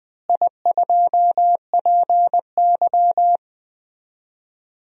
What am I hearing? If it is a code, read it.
Morse code, heard as a keyed tone, about 21 wpm, 700 Hz. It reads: I2PY
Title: DXCC-100 Call Signs - 4 Characters - Repeated 3x - 20wpm